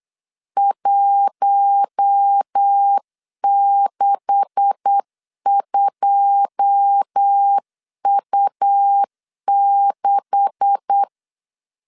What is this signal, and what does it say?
Morse code, heard as a keyed tone, about 8 wpm, 800 Hz. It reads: 162U6
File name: LEZIONI / 9 286 8wpm 3425